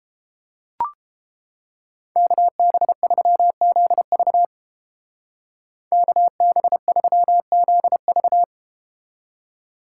Morse code, heard as a keyed tone, about 30 wpm, 700 Hz. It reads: KB3ZV KB3ZV